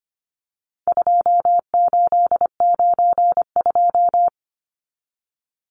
Morse code, heard as a keyed tone, about 25 wpm, 700 Hz. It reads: 2892
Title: Numbers - 4-Digit 25wpm